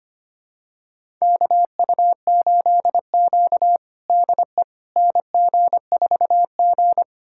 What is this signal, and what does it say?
Morse code, heard as a keyed tone, about 25 wpm, 700 Hz. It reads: KU8Q DE NG4G
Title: USA Callsign DE Callsign - 4 Characters - 25wpm